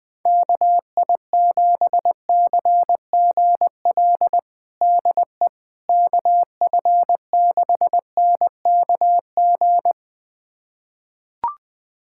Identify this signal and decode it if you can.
Morse code, heard as a keyed tone, about 20 wpm, 700 Hz. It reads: KI7CGL DE KF6NKG
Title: USA Callsign DE Callsign - 6 Characters - 20wpm